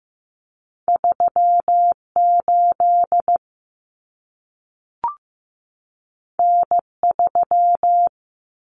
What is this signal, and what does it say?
Morse code, heard as a keyed tone, about 15 wpm, 700 Hz. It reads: 38 N3
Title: Common Cut Numbers - 2-Digit 15wpm